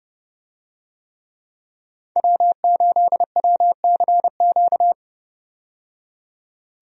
Morse code, heard as a keyed tone, about 30 wpm, 700 Hz. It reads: W8WCQ
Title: USA Call Signs - 5 Characters - 30wpm